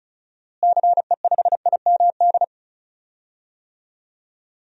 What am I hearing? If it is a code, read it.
Morse code, heard as a keyed tone, about 36 wpm, 700 Hz. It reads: CE5IMD
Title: DXCC-100 Call Signs - 6 Characters - Repeated 2x - 35wpm